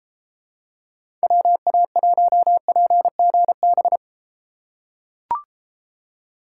Morse code, heard as a keyed tone, about 29 wpm, 700 Hz. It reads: WA1PGB